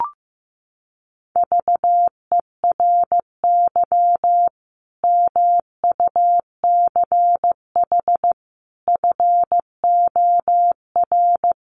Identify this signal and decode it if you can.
Morse code, heard as a keyed tone, about 15 wpm, 700 Hz. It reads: VERY MUCH FOR
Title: Sets of 5 Words 15wpm